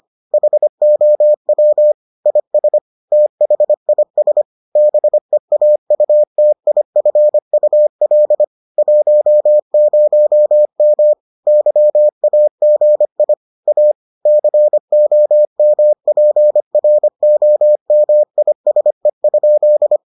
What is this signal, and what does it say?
Morse code, heard as a keyed tone, about 25 wpm, 600 Hz. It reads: HOW IS THIS BEAUTIFUL 10M YAGI A COMPROMISE?